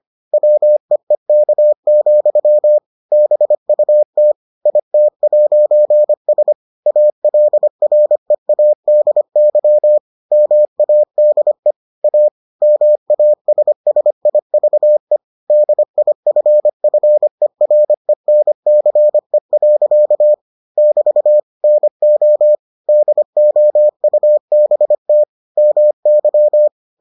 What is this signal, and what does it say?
Morse code, heard as a keyed tone, about 25 wpm, 600 Hz. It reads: WEEK, BUT IT'S ALREADY MADE A MASSIVE DIFFERENCE. <BT> NO DOUBT MY